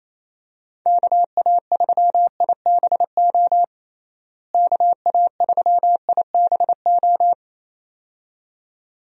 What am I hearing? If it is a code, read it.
Morse code, heard as a keyed tone, about 28 wpm, 700 Hz. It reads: KA3IBO KA3IBO
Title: USA Call Signs - 6 Characters - Repeated 2x - 28wpm